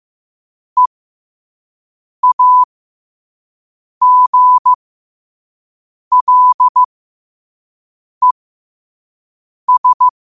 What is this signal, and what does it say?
Morse code, heard as a keyed tone, about 15 wpm, spaced as for 5 wpm, 1000 Hz. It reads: EAGLES
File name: qotdfiles / 5wpm